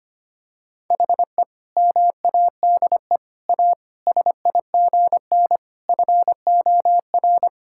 Morse code, heard as a keyed tone, about 25 wpm, 700 Hz. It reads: HE MADE A SIGN FOR
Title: Sentences from Top 1000 Words - Review Part 6 - 25wpm